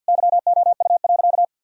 Morse code, heard as a keyed tone, about 50 wpm, 700 Hz. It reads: YOU.
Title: Sentences from Top 1000 Words - Review Part 3 - 50wpm